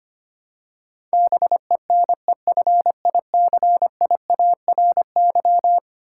Morse code, heard as a keyed tone, about 25 wpm, 700 Hz. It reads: BENEFICIARY
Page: morse-code-ninja-practice-25wpm